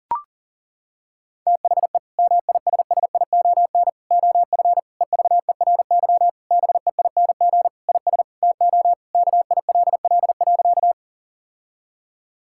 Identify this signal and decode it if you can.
Morse code, heard as a keyed tone, about 37 wpm, 700 Hz. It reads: THE MISSION OF EVERY BEING IS TO KILL.
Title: Sentences from Top 1600 Words 40wpm